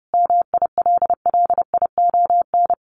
Morse code, heard as a keyed tone, about 30 wpm, 700 Hz. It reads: MILLION
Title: A Sound Of Thunder 30wpm